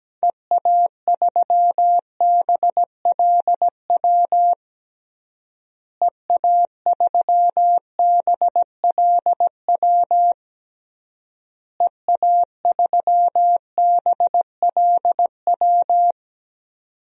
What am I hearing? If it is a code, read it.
Morse code, heard as a keyed tone, about 17 wpm, 700 Hz. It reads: EA3BLW EA3BLW EA3BLW
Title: DXCC-100 Call Signs - 6 Characters - Repeated 3x - 17wpm